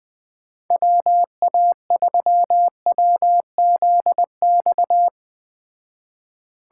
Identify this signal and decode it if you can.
Morse code, heard as a keyed tone, about 20 wpm, 700 Hz. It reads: WA3WZX